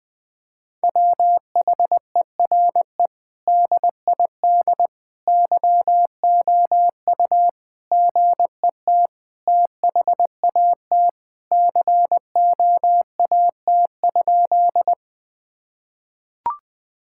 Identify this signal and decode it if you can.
Morse code, heard as a keyed tone, about 20 wpm, 700 Hz. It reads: WHERE DID YOU GET THAT COAT?
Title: Sentences from Top 1600 Words 20wpm